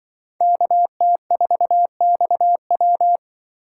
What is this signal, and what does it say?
Morse code, heard as a keyed tone, about 25 wpm, 700 Hz. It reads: KT4XW
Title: Speed Racing - USA 2x2 Call Signs 20wpm